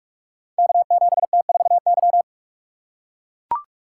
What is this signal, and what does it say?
Morse code, heard as a keyed tone, about 44 wpm, 700 Hz. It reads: K7T4Y
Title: Code Groups - 5 Letter:Numbers - 45wpm